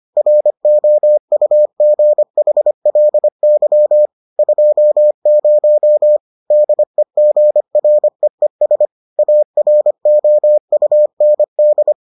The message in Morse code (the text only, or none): ROUGHLY 20 DEGREES AROUND